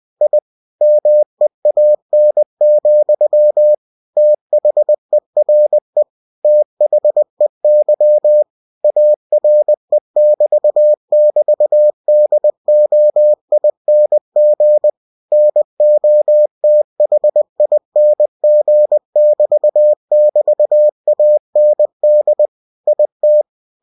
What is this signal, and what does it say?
Morse code, heard as a keyed tone, about 20 wpm, 600 Hz. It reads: I MEAN, THERE THEY ARE<BT><BT>DOING NOTHING<BT><BT>AND IT
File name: cw-books / winnie-the-pooh / winnie-20wpm-page-0019